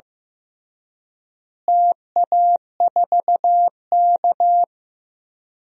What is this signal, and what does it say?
Morse code, heard as a keyed tone, about 15 wpm, 700 Hz. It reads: TA4K
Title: DXCC-100 Call Signs - 4 Characters - Repeated 2x - 15wpm